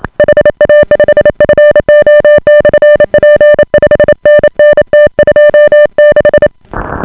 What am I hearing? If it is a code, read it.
Morse code, heard as a keyed tone, about 28 wpm, 590 Hz. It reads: HA5FO/P 5NN T26